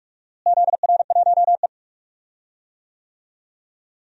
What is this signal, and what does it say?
Morse code, heard as a keyed tone, about 46 wpm, 700 Hz. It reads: ZR1E
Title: DXCC-100 Call Signs - 4 Characters - Repeated 2x - 45wpm